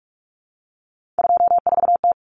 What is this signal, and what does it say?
Morse code, heard as a keyed tone, about 44 wpm, 700 Hz. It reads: 24T
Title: Common Cut Numbers - 3-Digit 45wpm